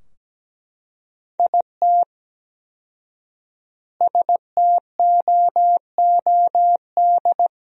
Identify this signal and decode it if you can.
Morse code, heard as a keyed tone, about 17 wpm, 700 Hz. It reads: IT STOOD
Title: 4x Word-Spacing - Sentences from Top 500 Words - 17wpm